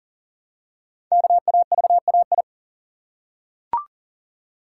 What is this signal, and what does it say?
Morse code, heard as a keyed tone, about 39 wpm, 700 Hz. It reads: KAVAI
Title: Code Groups - 5 Letters - 40wpm